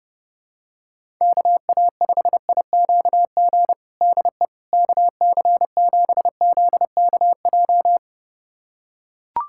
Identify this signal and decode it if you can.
Morse code, heard as a keyed tone, about 32 wpm, 700 Hz. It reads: KA5IQG DE KC7ZKJ
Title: USA Callsign DE Callsign - 6 Characters - 30wpm